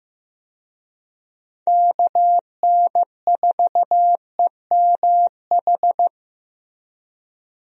Morse code, heard as a keyed tone, about 14 wpm, 700 Hz. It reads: KN4EMH
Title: USA Call Signs - 6 Characters - Repeated 3x - 15wpm